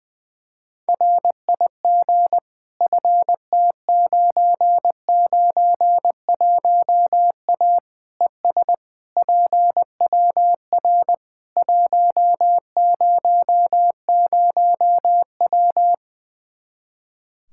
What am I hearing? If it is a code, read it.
Morse code, heard as a keyed tone, about 20 wpm, 700 Hz. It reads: RIG FT991A ES PWR 100W
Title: Practice QSO - Exchange 2 - Info - 20wpm